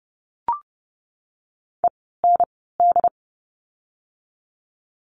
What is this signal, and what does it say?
Morse code, heard as a keyed tone, about 30 wpm, 700 Hz. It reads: E N D
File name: Lesson 16f - Letter c - Words 30wpm